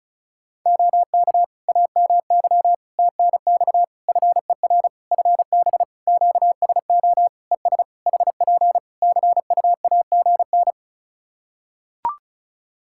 Morse code, heard as a keyed tone, about 34 wpm, 700 Hz. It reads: OK AMY TNX FER FB QSO ES HP CUAGN
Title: Practice QSO - Exchange 4 - Ending - 35wpm